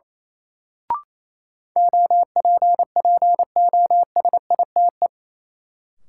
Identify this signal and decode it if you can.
Morse code, heard as a keyed tone, about 28 wpm, 700 Hz. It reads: OPPOSITE
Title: Sentences from Top 1300 Words - New Word Drill Alt - 28wpm